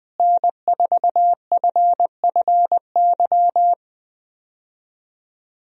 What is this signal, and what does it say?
Morse code, heard as a keyed tone, about 20 wpm, 700 Hz. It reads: N4FFY
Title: USA Call Signs - 5 Characters - Repeated 3x - 20wpm